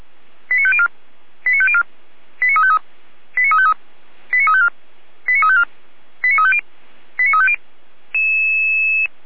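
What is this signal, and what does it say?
Five-tone selective call (ZVEI), heard as quick runs of five tones, tones of about 70 ms. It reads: 89504 89504 89353 89353 89354 89354 89350 89350
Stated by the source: ZVEI five-tone selective call 89235 received by SDR, served by the feuerwehr-orlamuende fire brigade.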